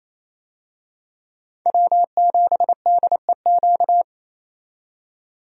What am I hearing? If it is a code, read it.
Morse code, heard as a keyed tone, about 28 wpm, 700 Hz. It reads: W7DEQ
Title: USA Call Signs - 5 Characters - 28wpm